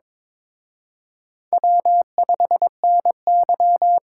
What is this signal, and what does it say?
Morse code, heard as a keyed tone, about 26 wpm, 700 Hz. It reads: W5NY
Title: Speed Racing - USA Call Signs 22wpm